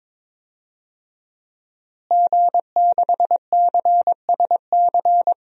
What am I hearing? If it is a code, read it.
Morse code, heard as a keyed tone, about 22 wpm, 700 Hz. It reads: G6CSC
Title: DXCC-100 Call Signs - 5 Characters - Repeated 3x - 22wpm